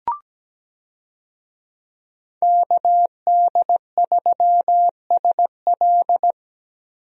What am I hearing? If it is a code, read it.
Morse code, heard as a keyed tone, about 17 wpm, 700 Hz. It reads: KD3SL
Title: USA Call Signs - 5 Characters - 17wpm